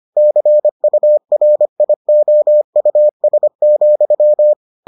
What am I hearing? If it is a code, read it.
Morse code, heard as a keyed tone, about 25 wpm, 600 Hz. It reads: CURIOUS,